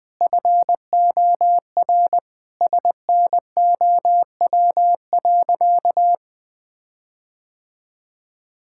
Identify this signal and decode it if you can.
Morse code, heard as a keyed tone, about 21 wpm, 700 Hz. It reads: FOR SNOW.
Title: Sentences with 4-Word Phrases - Spoken Beginning - 20wpm